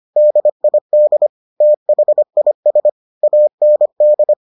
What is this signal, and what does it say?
Morse code, heard as a keyed tone, about 25 wpm, 600 Hz. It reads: DID THIS AND